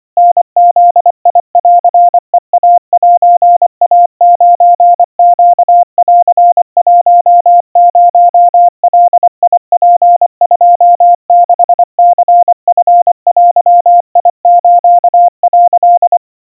Morse code (Japanese Wabun code), heard as a keyed tone, about 24 wpm, 700 Hz. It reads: タブンヘイセイ9ネン10ガツ26ニチデス」
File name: 90 cpm 700 Hz Crispy